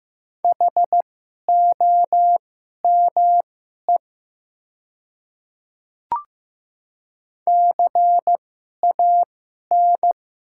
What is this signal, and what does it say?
Morse code, heard as a keyed tone, about 15 wpm, spaced as for 11 wpm, 700 Hz. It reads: HOME CAN
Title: 2x Character Spacing - Top 200 Words - 15wpm